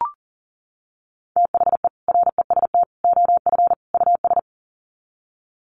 Morse code, heard as a keyed tone, about 40 wpm, 700 Hz. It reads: THE REST OF US
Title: Sets of 4 Words 40wpm